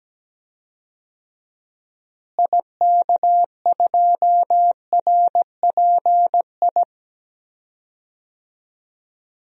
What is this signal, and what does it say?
Morse code, heard as a keyed tone, about 17 wpm, 700 Hz. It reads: IK2RPI
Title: DXCC-100 Call Signs - 6 Characters - Repeated 2x - 17wpm